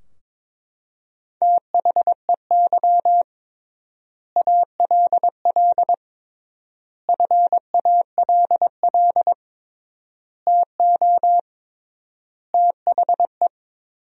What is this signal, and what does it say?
Morse code, heard as a keyed tone, about 21 wpm, 700 Hz. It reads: THEY ALL FALL TO THE
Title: 3x Word-Spacing - Sentences from Top 500 Words - 22wpm